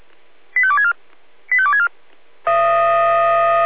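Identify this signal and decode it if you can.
Five-tone selective call (ZVEI), heard as quick runs of five tones, tones of about 70 ms. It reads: 85264 85264
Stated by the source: Received on an SDR; ZVEI five-tone call 85280.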